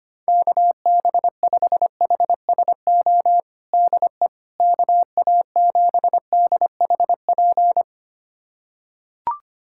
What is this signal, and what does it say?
Morse code, heard as a keyed tone, about 25 wpm, 700 Hz. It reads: KB5HSO DE KA7DHP
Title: USA Callsign DE Callsign 25wpm